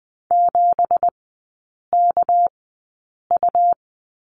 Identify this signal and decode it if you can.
Morse code, heard as a keyed tone, about 20 wpm, spaced as for 8 wpm, 700 Hz. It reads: 7KU
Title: 212 Lesson 34e - Number 8 - Three Characters 20wpm